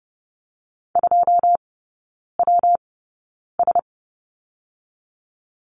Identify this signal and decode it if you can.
Morse code, heard as a keyed tone, about 30 wpm, 700 Hz. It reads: 2 W S
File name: Lesson 19e - Letter w - Three Characters 30wpm